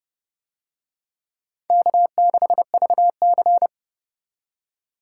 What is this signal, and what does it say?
Morse code, heard as a keyed tone, about 28 wpm, 700 Hz. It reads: K6VC